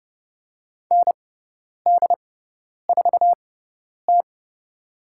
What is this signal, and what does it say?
Morse code, heard as a keyed tone, about 30 wpm, spaced as for 10 wpm, 700 Hz. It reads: ND4T
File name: Lesson 17g - Letter u - Call Signs 30wpm